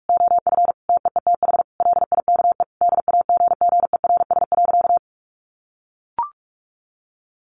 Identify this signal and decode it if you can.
Morse code, heard as a keyed tone, about 44 wpm, 700 Hz. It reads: OF TEETH LIKE DAGGERS.
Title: A Sound Of Thunder 45wpm